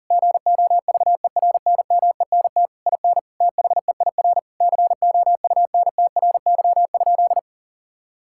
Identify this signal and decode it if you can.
Morse code, heard as a keyed tone, about 40 wpm, 700 Hz. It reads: GOVERNMENT IN THEIR COUNTRY?
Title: Sentences from Top 1000 Words - Review Part 5 - 40wpm